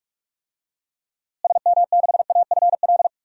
Morse code, heard as a keyed tone, about 46 wpm, 700 Hz. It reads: SM6AFL
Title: DXCC-100 Call Signs - 6 Characters - Repeated 3x - 45wpm